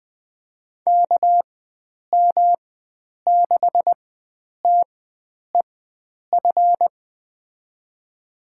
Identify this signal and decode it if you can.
Morse code, heard as a keyed tone, about 20 wpm, spaced as for 9 wpm, 700 Hz. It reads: KM6TEF
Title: USA Call Signs - 6 Characters - 4x Character Spacing - 20wpm